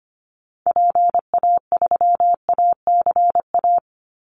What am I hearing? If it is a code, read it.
Morse code, heard as a keyed tone, about 25 wpm, 700 Hz. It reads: PA3ACA
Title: Call Signs - 25wpm